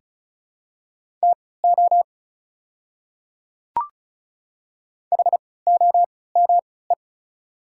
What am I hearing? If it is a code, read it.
Morse code, heard as a keyed tone, about 36 wpm, spaced as for 20 wpm, 690 Hz. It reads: TO HOME
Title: 3x Character Spacing - Top 500 Words - 35wpm